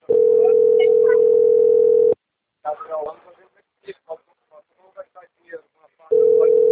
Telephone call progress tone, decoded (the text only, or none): ringback tone